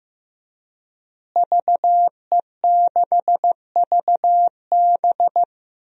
Morse code, heard as a keyed tone, about 15 wpm, 700 Hz. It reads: VE6VB
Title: DXCC-100 Call Signs - 5 Characters - Repeated 3x - 15wpm